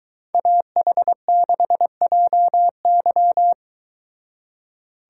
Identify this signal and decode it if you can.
Morse code, heard as a keyed tone, about 22 wpm, 700 Hz. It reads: AH6JY